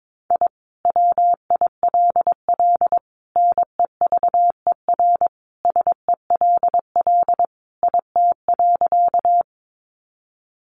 Morse code, heard as a keyed tone, about 22 wpm, 700 Hz. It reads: I WILL NEVER SELL IT.
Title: Sentences from Top 1100 Words 22wpm